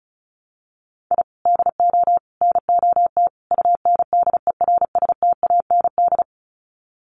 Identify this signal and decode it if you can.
Morse code, heard as a keyed tone, about 34 wpm, 700 Hz. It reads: I DO NOT UNDERSTAND
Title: Sets of 4 Words 35wpm